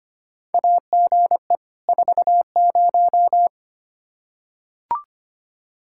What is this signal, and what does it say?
Morse code, heard as a keyed tone, about 25 wpm, 700 Hz. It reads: AGE 40
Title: Practice QSO - Exchange 3 - More Info - 25wpm